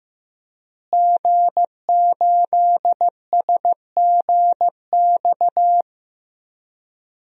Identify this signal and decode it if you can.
Morse code, heard as a keyed tone, about 15 wpm, 700 Hz. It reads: G8SGX
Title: DXCC-100 Call Signs - 5 Characters - Repeated 3x - 15wpm